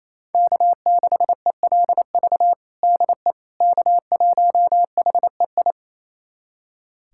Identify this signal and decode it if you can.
Morse code, heard as a keyed tone, about 27 wpm, 700 Hz. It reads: K6ELV DE K1HEI
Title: USA Callsign DE Callsign - 5 Characters - 28wpm